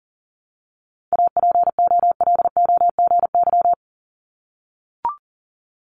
Morse code, heard as a keyed tone, about 42 wpm, 700 Hz. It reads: APOLOGY
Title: Multi-letter Phonemes 40wpm